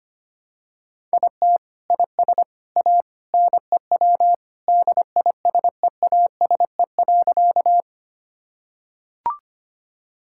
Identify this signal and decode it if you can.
Morse code, heard as a keyed tone, about 25 wpm, 700 Hz. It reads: IT IS A NEW DISEASE.
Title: Sentences from Top 1000 Words - Review Part 6 - 25wpm